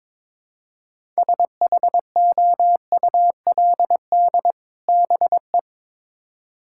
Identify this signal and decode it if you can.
Morse code, heard as a keyed tone, about 22 wpm, 700 Hz. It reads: SHOULD BE
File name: Build A Phrase - 2-Word Phrases - 22wpm